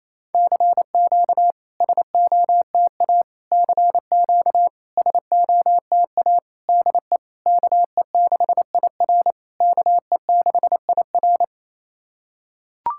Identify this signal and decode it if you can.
Morse code, heard as a keyed tone, about 28 wpm, 690 Hz. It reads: CQ SOTA CQ SOTA DE KE6IR KE6IR
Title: SOTA QSO Practice 28wpm